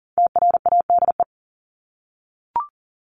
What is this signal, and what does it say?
Morse code, heard as a keyed tone, about 40 wpm, 700 Hz. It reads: TRADE
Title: Top 501-750 Words 40wpm